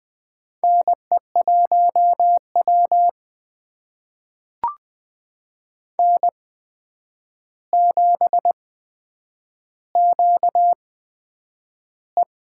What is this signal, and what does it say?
Morse code, heard as a keyed tone, about 20 wpm, spaced as for 9 wpm, 700 Hz. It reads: NE1W N7QE
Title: USA Call Signs - 4 Characters - 8x Character Spacing - 20wpm